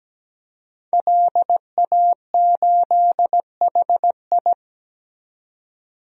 Morse code, heard as a keyed tone, about 17 wpm, 700 Hz. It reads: LA8HI
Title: DXCC-100 Call Signs - 5 Characters - 17wpm